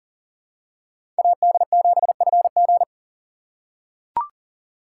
Morse code, heard as a keyed tone, about 40 wpm, 700 Hz. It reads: AD7FG
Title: USA Call Signs - 5 Characters - 2x Character Spacing - 40wpm